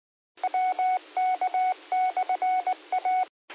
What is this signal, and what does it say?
Morse code, heard as a keyed tone, about 19 wpm, 740 Hz. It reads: WK/A